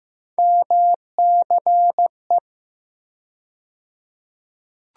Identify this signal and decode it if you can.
Morse code, heard as a keyed tone, about 15 wpm, 700 Hz. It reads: MCE
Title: USA Call Sign Postfixes - 3 Letters 15wpm